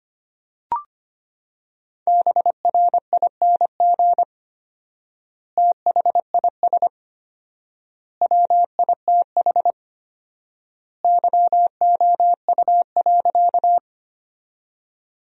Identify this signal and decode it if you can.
Morse code, heard as a keyed tone, about 25 wpm, 700 Hz. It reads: BRING THIS WITH YOU.
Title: 4x Word-Spacing - Sentences from Top 500 Words - 25wpm